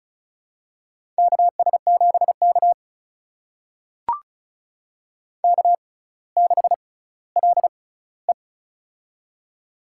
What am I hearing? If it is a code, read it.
Morse code, heard as a keyed tone, about 36 wpm, spaced as for 18 wpm, 700 Hz. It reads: KS7K K6LE
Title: USA Call Signs - 4 Characters - 6x Character Spacing - 35wpm